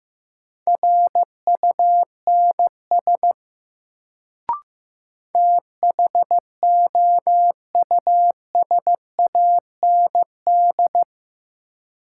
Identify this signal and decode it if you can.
Morse code, heard as a keyed tone, about 15 wpm, 700 Hz. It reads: RUNS THOUSAND
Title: Sentences from Top 600 Words - New Word Drill - 15wpm